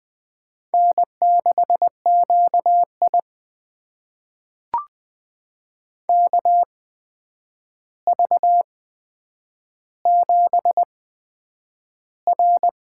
Morse code, heard as a keyed tone, about 20 wpm, spaced as for 9 wpm, 700 Hz. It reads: N6QI KV7R
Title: USA Call Signs - 4 Characters - 8x Character Spacing - 20wpm